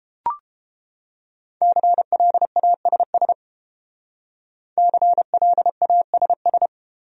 Morse code, heard as a keyed tone, about 28 wpm, 700 Hz. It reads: CLASS CLASS